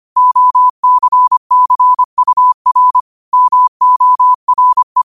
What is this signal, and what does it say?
Morse code, heard as a keyed tone, about 25 wpm, 1000 Hz. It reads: OCCUR MORE
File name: qotdfiles / 25wpm